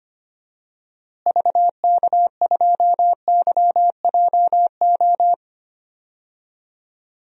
Morse code, heard as a keyed tone, about 25 wpm, 700 Hz. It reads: VK2YJO
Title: DXCC-100 Call Signs - 6 Characters - Repeated 2x - 25wpm